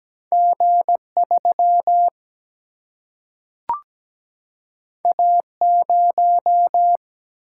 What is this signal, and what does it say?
Morse code, heard as a keyed tone, about 17 wpm, 700 Hz. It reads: G 3 A 0
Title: Single-Letter Single-Number 17wpm